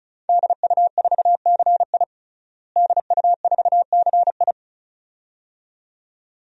Morse code, heard as a keyed tone, about 34 wpm, 700 Hz. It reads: DU4CI DU4CI
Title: DXCC-100 Call Signs - 5 Characters - Repeated 2x - 35wpm